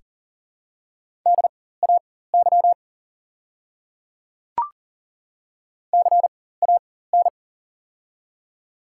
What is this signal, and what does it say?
Morse code, heard as a keyed tone, about 40 wpm, spaced as for 19 wpm, 700 Hz. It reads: DAY CAN
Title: 4x Character Spacing - Top 100 Words - 40wpm